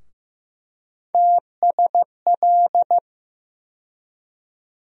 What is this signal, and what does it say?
Morse code, heard as a keyed tone, about 16 wpm, 700 Hz. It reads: TSL
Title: USA Call Sign Postfixes - 3 Letters 15wpm